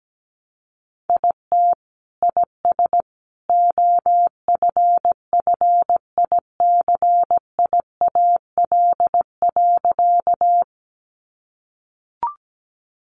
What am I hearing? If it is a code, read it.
Morse code, heard as a keyed tone, about 17 wpm, 700 Hz. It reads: IT IS OFFICIAL.
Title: Sentences from Top 1000 Words 17wpm